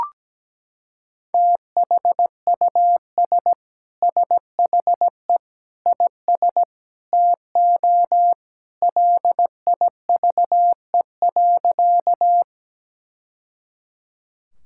Morse code, heard as a keyed tone, about 17 wpm, 700 Hz. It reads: THUS SHE IS TO LIVE.
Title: Sentences from Top 1000 Words - Review Part 5 - 17wpm